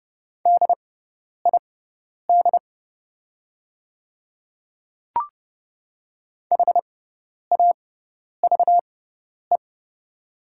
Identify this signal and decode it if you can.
Morse code, heard as a keyed tone, about 29 wpm, spaced as for 10 wpm, 700 Hz. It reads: DID HAVE
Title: 6x Character Spacing - Top 100 Words - 30wpm